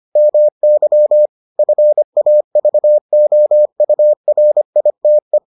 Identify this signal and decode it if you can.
Morse code, heard as a keyed tone, about 25 wpm, 600 Hz. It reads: MY FAVOURITE